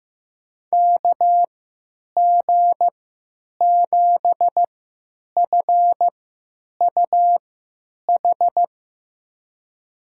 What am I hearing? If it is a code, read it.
Morse code, heard as a keyed tone, about 15 wpm, spaced as for 9 wpm, 700 Hz. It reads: KG7FUH
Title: USA Call Signs - 6 Characters - 3x Character Spacing - 15wpm